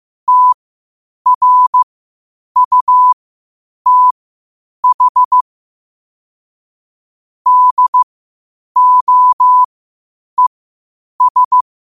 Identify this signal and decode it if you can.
Morse code, heard as a keyed tone, about 15 wpm, spaced as for 8 wpm, 1000 Hz. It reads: TRUTH DOES
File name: qotdfiles / 8wpm